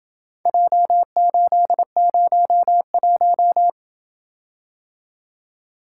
Morse code, heard as a keyed tone, about 28 wpm, 700 Hz. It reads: J801